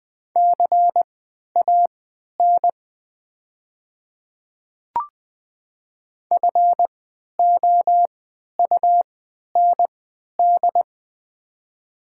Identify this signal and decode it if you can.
Morse code, heard as a keyed tone, about 20 wpm, spaced as for 11 wpm, 700 Hz. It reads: CAN FOUND
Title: 3x Character Spacing - Top 300 Words - 20wpm